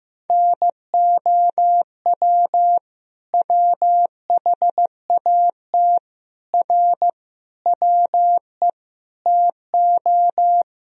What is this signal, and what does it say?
Morse code, heard as a keyed tone, about 15 wpm, 700 Hz. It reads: NOW WHAT R WE TO
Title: Sentences from Top 100 Words and Abbreviations 15wpm